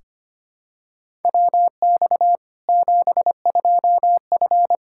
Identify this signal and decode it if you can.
Morse code, heard as a keyed tone, about 25 wpm, 700 Hz. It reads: WX 72F
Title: Practice QSO - Exchange 2 - Info - 25wpm